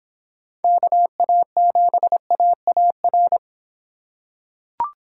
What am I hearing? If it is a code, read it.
Morse code, heard as a keyed tone, about 23 wpm, 700 Hz. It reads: KA7AAR